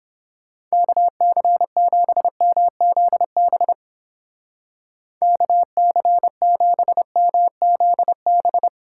Morse code, heard as a keyed tone, about 30 wpm, 700 Hz. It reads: KC7MZB KC7MZB